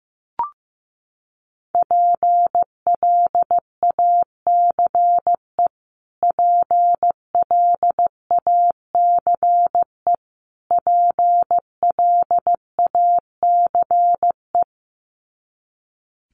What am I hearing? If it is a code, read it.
Morse code, heard as a keyed tone, about 15 wpm, 700 Hz. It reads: PLACE PLACE PLACE
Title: Top 500 Words Repeated 15wpm